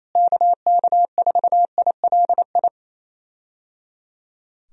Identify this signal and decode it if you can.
Morse code, heard as a keyed tone, about 27 wpm, 700 Hz. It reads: KK4ILI
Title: USA Call Signs - 6 Characters - Repeated 3x - 28wpm